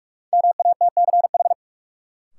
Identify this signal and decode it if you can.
Morse code, heard as a keyed tone, about 42 wpm, 700 Hz. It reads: MATCH
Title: Sentences from Top 1600 Words - New Word Drill - 45wpm